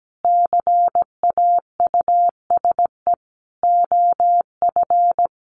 Morse code, heard as a keyed tone, about 17 wpm, 700 Hz. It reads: CAUSE OF
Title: Sentences from Top 1100 Words 17wpm